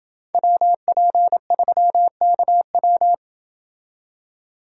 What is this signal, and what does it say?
Morse code, heard as a keyed tone, about 27 wpm, 700 Hz. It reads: WP3KW